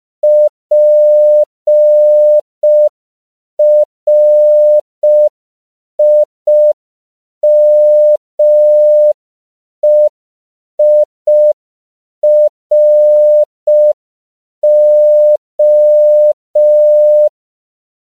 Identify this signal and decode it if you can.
Morse code, heard as a keyed tone, about 5 wpm, 600 Hz. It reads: PRIMEIRO
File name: CW / 5PPM